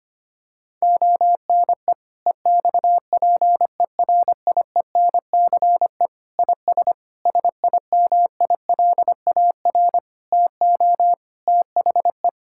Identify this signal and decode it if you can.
Morse code, heard as a keyed tone, about 25 wpm, 700 Hz. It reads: ONE EXPERIENCE IS SIMILAR TO THE
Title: Sentences from Top 1000 Words - Review Part 5 - 25wpm